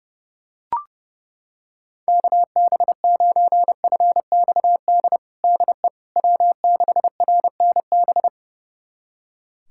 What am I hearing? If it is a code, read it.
Morse code, heard as a keyed tone, about 30 wpm, 700 Hz. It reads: KB9FXD DE W6RNB
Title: USA Callsign DE Callsign 30wpm